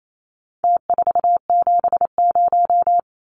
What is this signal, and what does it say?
Morse code, heard as a keyed tone, about 28 wpm, 700 Hz. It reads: T470
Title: Numbers - Contest Style - 4-Digit 28wpm